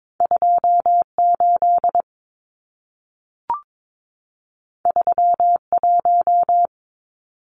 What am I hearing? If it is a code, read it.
Morse code, heard as a keyed tone, about 22 wpm, 700 Hz. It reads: 28 31
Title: Numbers - 2-Digit 22wpm